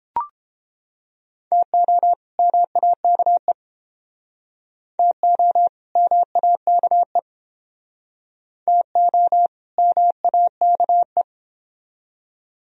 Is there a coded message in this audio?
Yes: Morse code from 1.5 to 11.2 s, heard as a keyed tone, about 30 wpm, 700 Hz. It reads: TO MAKE TO MAKE TO MAKE